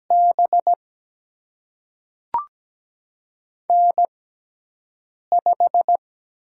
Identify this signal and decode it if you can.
Morse code, heard as a keyed tone, about 17 wpm, spaced as for 6 wpm, 700 Hz. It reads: B N5
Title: USA Call Signs - 6x Character Spacing - 17wpm